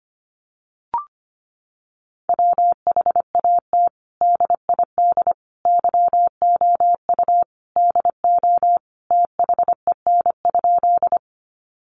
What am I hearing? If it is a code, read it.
Morse code, heard as a keyed tone, about 25 wpm, 700 Hz. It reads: WHAT DID YOU DO THEN?
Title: Sentences from Top 100 Words 25wpm